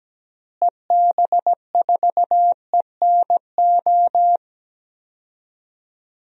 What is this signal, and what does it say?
Morse code, heard as a keyed tone, about 17 wpm, 700 Hz. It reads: EB4ENO
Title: DXCC-100 Call Signs - 6 Characters - Repeated 2x - 17wpm